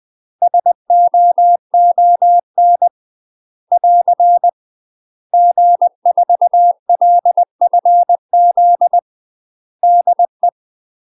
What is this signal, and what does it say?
Morse code, heard as a keyed tone, about 20 wpm, 700 Hz. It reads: SOON <AR> G4LFZ DE
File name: UK MASTER QSOs TRAINING / 20WPM / UK MASTER QSOs 0042